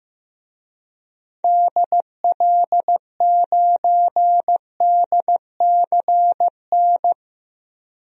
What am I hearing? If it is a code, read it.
Morse code, heard as a keyed tone, about 15 wpm, 700 Hz. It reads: DL9DCN
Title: DXCC-100 Call Signs - 6 Characters - Repeated 3x - 15wpm